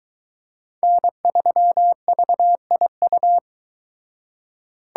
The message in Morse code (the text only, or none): N3VIU